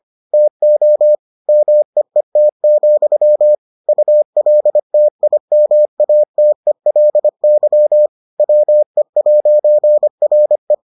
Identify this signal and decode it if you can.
Morse code, heard as a keyed tone, about 25 wpm, 600 Hz. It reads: TO MEET, ULTIMATELY WE'RE